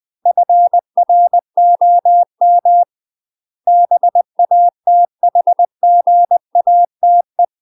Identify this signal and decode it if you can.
Morse code, heard as a keyed tone, about 20 wpm, 700 Hz. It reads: FROM BATHGATE
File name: UK MASTER QSOs TRAINING / 20WPM / UK MASTER QSOs 0003